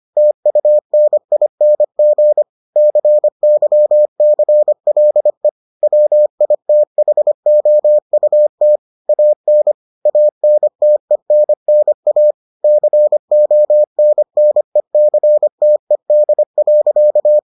Morse code, heard as a keyed tone, about 25 wpm, 600 Hz. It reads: TUNING CYCLE WITHOUT AN ANTENNA CONNECTED.